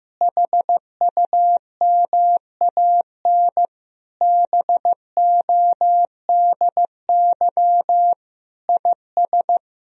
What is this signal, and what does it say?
Morse code, heard as a keyed tone, about 15 wpm, 700 Hz. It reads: HUMAN BODY IS
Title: Sentences from Top 1000 Words - Review Part 3 - 15wpm